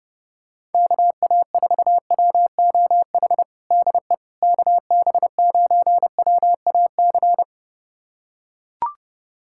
Morse code, heard as a keyed tone, about 32 wpm, 700 Hz. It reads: KA4WOH DE KB9WAC
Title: USA Callsign DE Callsign 30wpm